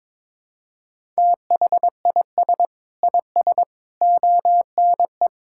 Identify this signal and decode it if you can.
Morse code, heard as a keyed tone, about 22 wpm, 700 Hz. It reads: THIS IS ONE
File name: Sentences with 7-Word Phrases - 22wpm